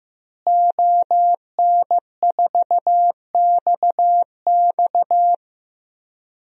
Morse code, heard as a keyed tone, about 15 wpm, 700 Hz. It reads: ON4XX
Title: DXCC-100 Call Signs - 5 Characters - Repeated 2x - 15wpm